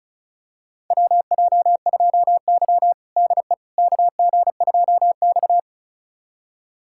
Morse code, heard as a keyed tone, about 36 wpm, 700 Hz. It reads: WJ2Y DE KG2X
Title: USA Callsign DE Callsign - 4 Characters - 35wpm